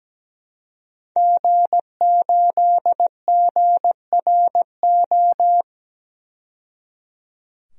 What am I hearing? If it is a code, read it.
Morse code, heard as a keyed tone, about 17 wpm, 700 Hz. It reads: G8GRO
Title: DXCC-100 Call Signs - 5 Characters - 17wpm